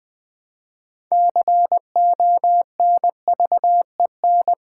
Morse code, heard as a keyed tone, about 20 wpm, 700 Hz. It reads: CONVEN
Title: Build A Word - 6 Letters At A Time - 12-Letter Words - 20wpm